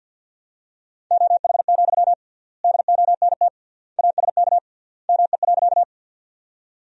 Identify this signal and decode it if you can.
Morse code, heard as a keyed tone, about 52 wpm, 700 Hz. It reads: OH, DONT ASK ME.
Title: 3x Word-Spacing - Sentences from Top 500 Words - 50wpm